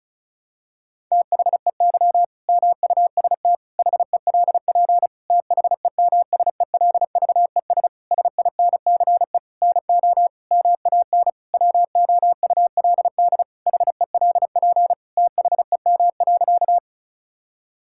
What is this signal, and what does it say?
Morse code, heard as a keyed tone, about 36 wpm, 700 Hz. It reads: THEY MUST HELP THEMSELVES SINCE NO MAN WOULD HELP THEM.
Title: Sentences from Top 1000 Words - Review Part 5 - 35wpm